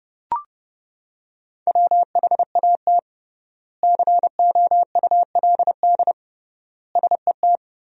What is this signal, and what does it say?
Morse code, heard as a keyed tone, about 30 wpm, 700 Hz. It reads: WHAT COULD SET
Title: 3x Word-Spacing - Sentences from Top 200 Words - 30wpm